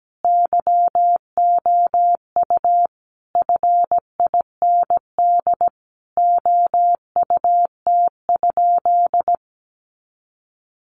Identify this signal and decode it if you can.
Morse code, heard as a keyed tone, about 17 wpm, 700 Hz. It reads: YOU FIND OUT?
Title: Sentences from Top 100 Words 17wpm